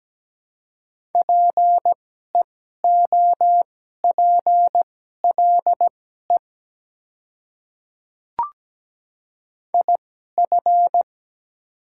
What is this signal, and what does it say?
Morse code, heard as a keyed tone, about 17 wpm, spaced as for 12 wpm, 700 Hz. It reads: PEOPLE IF
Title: 2x Character Spacing - Top 300 Words - 17wpm